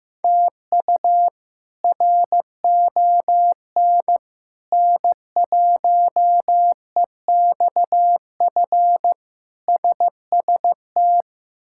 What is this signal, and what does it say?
Morse code, heard as a keyed tone, about 15 wpm, 700 Hz. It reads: TU RON N1EXF SST
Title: K1USN SST QSO Practice - 15wpm